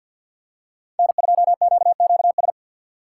Morse code, heard as a keyed tone, about 52 wpm, 700 Hz. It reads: N2QQS
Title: USA Call Signs - 5 Characters - 6x Character Spacing - 50wpm